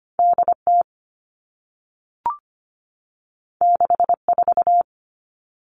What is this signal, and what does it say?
Morse code, heard as a keyed tone, about 25 wpm, 700 Hz. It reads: DT 64
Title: All Cut Numbers - 2-Digit 25wpm